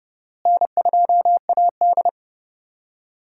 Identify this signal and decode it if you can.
Morse code, heard as a keyed tone, about 32 wpm, 700 Hz. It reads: N2AD
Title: USA Call Signs - 4 Characters - Repeated 3x - 30wpm